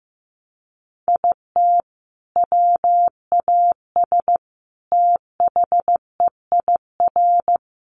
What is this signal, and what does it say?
Morse code, heard as a keyed tone, about 14 wpm, 700 Hz. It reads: IT WAS THEIR
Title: Sentences from Top 1200 Words 15wpm